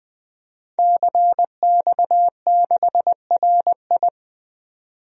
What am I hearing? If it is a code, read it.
Morse code, heard as a keyed tone, about 20 wpm, 700 Hz. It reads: CX6RI
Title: DXCC-100 Call Signs - 5 Characters - 20wpm